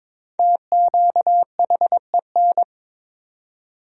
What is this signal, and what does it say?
Morse code, heard as a keyed tone, about 22 wpm, 700 Hz. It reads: TQHEN
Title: 4x Word-Spacing - Sentences from Top 100 Words - 22wpm